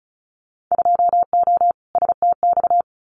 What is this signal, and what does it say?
Morse code, heard as a keyed tone, about 34 wpm, 700 Hz. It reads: 2O STX
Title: Winter Field Day Exchanges 35wpm